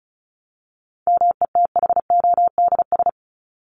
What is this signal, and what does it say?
Morse code, heard as a keyed tone, about 34 wpm, 700 Hz. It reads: METHODS
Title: ES:S Suffix - Top 100 Words with Base Words 35wpm